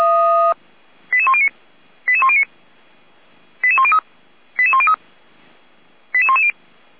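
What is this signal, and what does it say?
Five-tone selective call (ZVEI), heard as quick runs of five tones, tones of about 70 ms. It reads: 88109 88109 88103 88103 88110
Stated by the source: ZVEI five-tone call 88192 received by SDR, fire brigade Feuerwehr Orlamünde (f85360).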